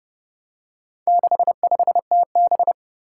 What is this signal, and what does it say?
Morse code, heard as a keyed tone, about 30 wpm, 700 Hz. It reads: 65TB
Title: Four Challenging Letter-Number Combinations 30wpm